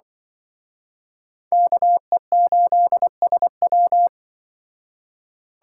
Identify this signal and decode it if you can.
Morse code, heard as a keyed tone, about 24 wpm, 700 Hz. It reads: KE8SW
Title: Speed Racing - USA Call Signs 20wpm